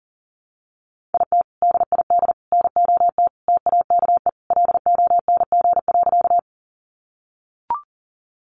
Morse code, heard as a keyed tone, about 40 wpm, 700 Hz. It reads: IT DID NOT TAKE LONG.
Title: Sentences from Top 600 Words 40wpm